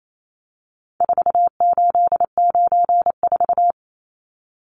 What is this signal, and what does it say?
Morse code, heard as a keyed tone, about 28 wpm, 700 Hz. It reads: 4894
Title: Numbers - 4-Digit 28wpm